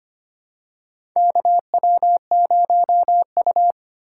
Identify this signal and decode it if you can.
Morse code, heard as a keyed tone, about 25 wpm, 700 Hz. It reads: KW0U
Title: USA Call Signs - 4 Characters - 4x Character Spacing - 25wpm